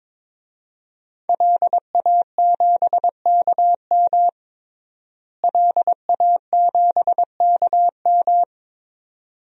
Morse code, heard as a keyed tone, about 22 wpm, 700 Hz. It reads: LA7KM LA7KM
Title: DXCC-100 Call Signs - 5 Characters - Repeated 2x - 22wpm